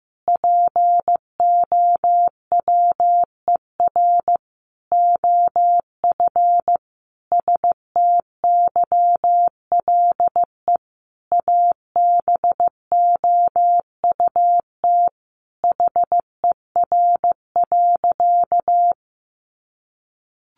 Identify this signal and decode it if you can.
Morse code, heard as a keyed tone, about 15 wpm, 700 Hz. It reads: POWER OF STYLE ABOUT HER.
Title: Sentences from Top 800 Words 15wpm